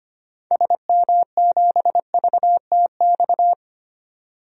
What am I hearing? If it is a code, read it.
Morse code, heard as a keyed tone, about 25 wpm, 700 Hz. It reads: SM7VTX
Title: DXCC-100 Call Signs - 6 Characters - 25wpm